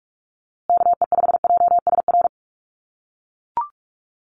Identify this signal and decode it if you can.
Morse code, heard as a keyed tone, about 44 wpm, 700 Hz. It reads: KE5JSR
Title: USA Call Signs Repeated 3x 45wpm